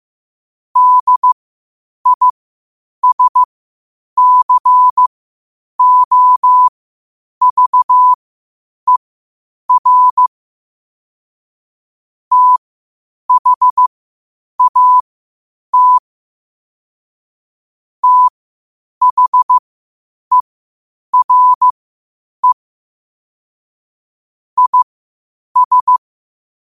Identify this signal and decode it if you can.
Morse code, heard as a keyed tone, about 15 wpm, spaced as for 8 wpm, 1000 Hz. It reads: DISCOVER THAT THERE IS